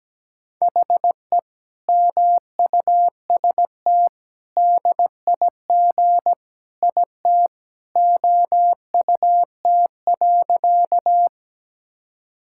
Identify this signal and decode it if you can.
Morse code, heard as a keyed tone, about 17 wpm, 700 Hz. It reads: HE MUST DIG IT OUT.
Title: Sentences from Top 1600 Words 17wpm